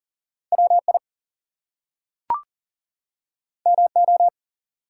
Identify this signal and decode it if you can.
Morse code, heard as a keyed tone, about 40 wpm, 700 Hz. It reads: WI MO
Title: US States and Canadian Provinces 40wpm